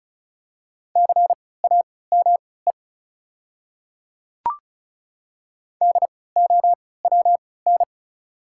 Morse code, heard as a keyed tone, about 36 wpm, spaced as for 20 wpm, 690 Hz. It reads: CAME DOWN
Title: 3x Character Spacing - Top 200 Words - 35wpm